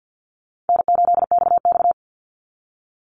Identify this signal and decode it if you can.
Morse code, heard as a keyed tone, about 50 wpm, 700 Hz. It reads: N8XX
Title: Call Signs Repeated 2x - 50wpm